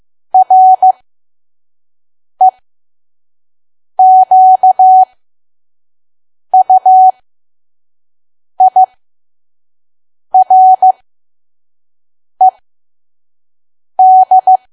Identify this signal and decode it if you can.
Morse code, heard as a keyed tone, about 15 wpm, spaced as for 5 wpm, 750 Hz. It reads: REQUIRED
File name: Archive / 5 WPM files / PM